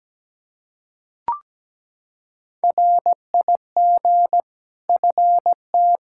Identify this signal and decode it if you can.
Morse code, heard as a keyed tone, about 17 wpm, 700 Hz. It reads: RIG FT
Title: K1USN SST QSO Practice 17wpm